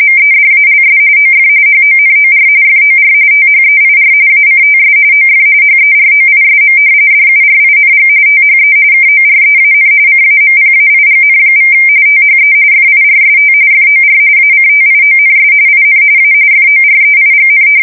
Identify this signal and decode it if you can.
RTTY, about 45.45 baud. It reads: MEDALS OF HONOR FOR THEIR ACTS OF HEROISM DURING THE KOREAN CAMPAIGN= THE FIRST FOR ACTIONS IN A FOREIGN
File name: fsk170 / usnavy3